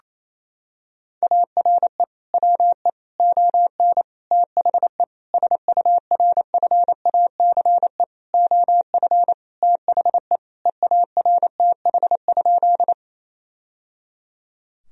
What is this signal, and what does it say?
Morse code, heard as a keyed tone, about 28 wpm, 700 Hz. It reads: ARE WE ON THE SURFACE OF THE EARTH?
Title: Sentences from Top 1000 Words - Review Part 3 - 28wpm